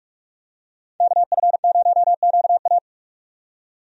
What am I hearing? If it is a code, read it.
Morse code, heard as a keyed tone, about 46 wpm, 700 Hz. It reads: KF0QA
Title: USA Call Signs - 5 Characters - 45wpm